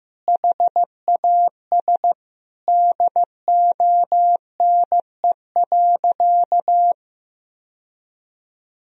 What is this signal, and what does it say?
Morse code, heard as a keyed tone, about 16 wpm, 700 Hz. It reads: HAS DONE.
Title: Sentences from Top 1000 Words - Review Part 1 - 15wpm